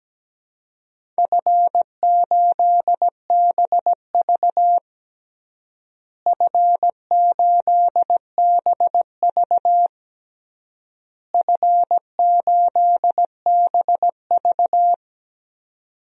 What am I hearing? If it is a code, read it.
Morse code, heard as a keyed tone, about 17 wpm, 700 Hz. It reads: F8BV F8BV F8BV
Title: DXCC-100 Call Signs - 4 Characters - Repeated 3x - 17wpm